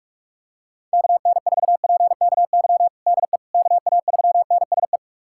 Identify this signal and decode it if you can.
Morse code, heard as a keyed tone, about 41 wpm, 700 Hz. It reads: KN4PKY DE KA3NSE
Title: USA Callsign DE Callsign - 6 Characters - 45wpm